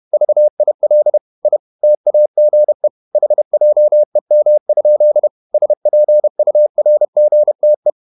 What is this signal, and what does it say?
Morse code, heard as a keyed tone, about 31 wpm, 600 Hz. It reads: VIL I TAGE HJEM? SPURGTE
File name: Ferie QRQ 31wpm